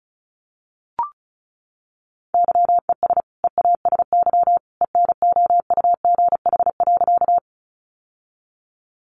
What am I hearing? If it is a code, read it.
Morse code, heard as a keyed tone, about 34 wpm, 700 Hz. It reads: YES EASY ENOUGH.
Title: Sentences from Top 700 Words 35wpm